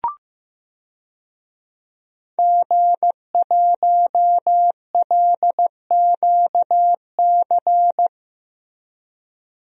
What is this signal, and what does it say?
Morse code, heard as a keyed tone, about 15 wpm, 700 Hz. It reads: G1LQC
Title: DXCC-100 Call Signs - 5 Characters - 15wpm